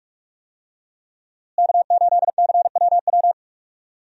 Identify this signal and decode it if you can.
Morse code, heard as a keyed tone, about 41 wpm, 700 Hz. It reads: K8CWW
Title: USA Call Signs - 5 Characters - Repeated 2x - 45wpm